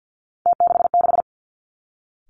Morse code, heard as a keyed tone, about 50 wpm, 700 Hz. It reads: T66
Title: Three Challenging Letter-Number Combinations 50wpm